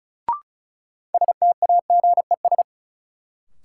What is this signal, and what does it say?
Morse code, heard as a keyed tone, about 34 wpm, 700 Hz. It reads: STAGES
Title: Sets of 5 Words Encore - New Word Drill - 35wpm